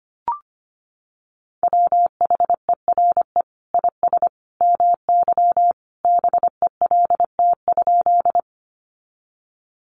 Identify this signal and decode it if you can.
Morse code, heard as a keyed tone, about 25 wpm, 700 Hz. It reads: WHERE IS MY BELT?
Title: Sentences from Top 1400 Words 25wpm